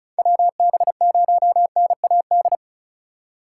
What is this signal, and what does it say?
Morse code, heard as a keyed tone, about 36 wpm, 700 Hz. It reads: WB0NAD
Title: USA Call Signs - 6 Characters - 2x Character Spacing - 35wpm